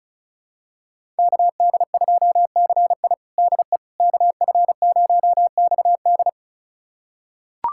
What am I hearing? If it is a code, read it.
Morse code, heard as a keyed tone, about 34 wpm, 700 Hz. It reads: KD2CI DE KF0XD
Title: USA Callsign DE Callsign - 5 Characters - 35wpm